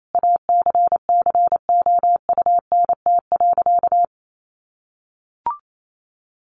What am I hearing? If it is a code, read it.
Morse code, heard as a keyed tone, about 28 wpm, 700 Hz. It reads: ACCOUNT.
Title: Sentences from Top 1400 Words 28wpm